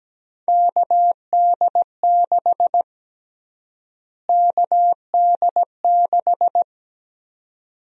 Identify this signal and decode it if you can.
Morse code, heard as a keyed tone, about 17 wpm, 700 Hz. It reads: KD6 KD6
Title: USA Call Sign Prefixes - Repeated 2x - 17wpm